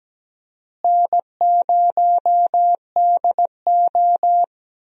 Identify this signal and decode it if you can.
Morse code, heard as a keyed tone, about 17 wpm, 700 Hz. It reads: N0DO
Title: USA Call Signs - 4 Characters - Repeated 2x - 17wpm